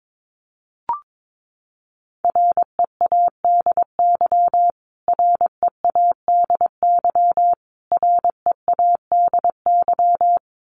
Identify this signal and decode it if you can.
Morse code, heard as a keyed tone, about 22 wpm, 700 Hz. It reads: READY READY READY
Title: Top 500 Words Repeated 22wpm